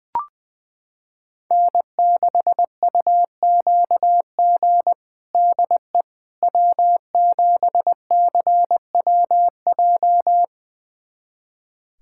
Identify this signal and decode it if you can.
Morse code, heard as a keyed tone, about 20 wpm, 700 Hz. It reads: N6UQG DE W7CWJ
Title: USA Callsign DE Callsign - 5 Characters - 20wpm